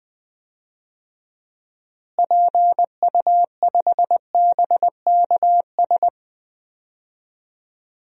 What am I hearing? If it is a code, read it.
Morse code, heard as a keyed tone, about 21 wpm, 700 Hz. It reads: PU5BKS
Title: DXCC-100 Call Signs - 6 Characters - Repeated 3x - 20wpm